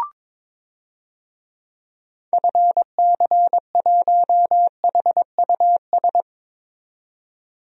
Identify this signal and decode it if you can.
Morse code, heard as a keyed tone, about 21 wpm, 700 Hz. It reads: FC1HUS
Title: DXCC-100 Call Signs - 6 Characters - 22wpm